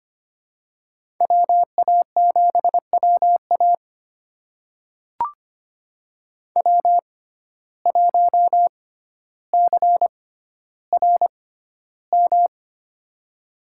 Morse code, heard as a keyed tone, about 25 wpm, spaced as for 13 wpm, 700 Hz. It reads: WA7WA W1CRM
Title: USA Call Signs - 5 Characters - 6x Character Spacing - 25wpm